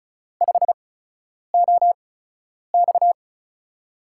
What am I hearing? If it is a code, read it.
Morse code, heard as a keyed tone, about 36 wpm, spaced as for 10 wpm, 700 Hz. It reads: 5OX
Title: USA Call Signs - 5 Characters - 8x Character Spacing - 35wpm